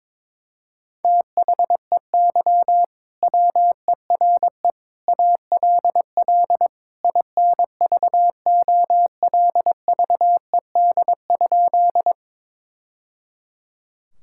Morse code, heard as a keyed tone, about 21 wpm, 700 Hz. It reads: THEY WERE ALL INVOLVED?
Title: Sentences from Top 1000 Words - Review Part 5 - 22wpm